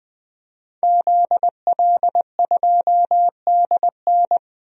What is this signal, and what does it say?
Morse code, heard as a keyed tone, about 21 wpm, 700 Hz. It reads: ZL2DN
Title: DXCC-100 Call Signs - 5 Characters - Repeated 2x - 20wpm